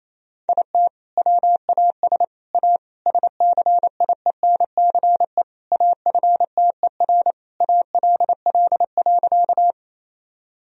Morse code, heard as a keyed tone, about 28 wpm, 700 Hz. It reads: IT WAS A SCIENCE AFTER ALL.
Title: Sentences from Top 1000 Words - Review Part 2 - 28wpm